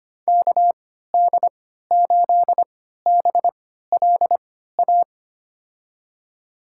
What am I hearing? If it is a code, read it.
Morse code, heard as a keyed tone, about 25 wpm, spaced as for 14 wpm, 700 Hz. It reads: KD8BLA
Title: USA Call Signs - 6 Characters - 3x Character Spacing - 25wpm